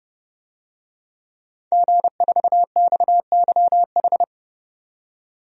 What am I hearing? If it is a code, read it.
Morse code, heard as a keyed tone, about 30 wpm, 700 Hz. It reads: G4XYH
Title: DXCC-100 Call Signs - 5 Characters - 30wpm